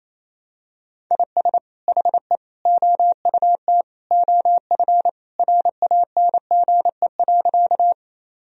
Morse code, heard as a keyed tone, about 28 wpm, 700 Hz. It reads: IS HE OUT OF RANGE.
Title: Sentences from Top 1000 Words - Review Part 4 - 28wpm